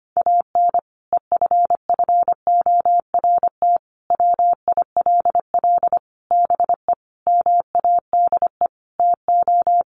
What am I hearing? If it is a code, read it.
Morse code, heard as a keyed tone, about 25 wpm, 700 Hz. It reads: AN EFFORT WILL BE MADE TO
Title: Sentences from Top 1500 Words 25wpm